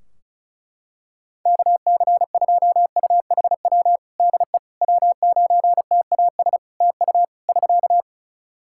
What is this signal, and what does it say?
Morse code, heard as a keyed tone, about 36 wpm, 700 Hz. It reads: KC2UHW DE W9TAS TU <SK>
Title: Practice QSO - Exchange 4 - Ending - 35wpm